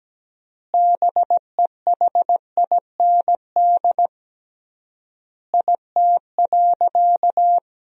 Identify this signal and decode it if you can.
Morse code, heard as a keyed tone, about 17 wpm, 700 Hz. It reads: BEHIND IT.
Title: 3x Word-Spacing - Sentences from Top 500 Words - 17wpm